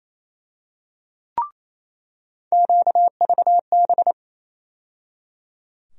Morse code, heard as a keyed tone, about 28 wpm, 700 Hz. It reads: QVB
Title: USA Call Sign Postfixes - 3 Letters 28wpm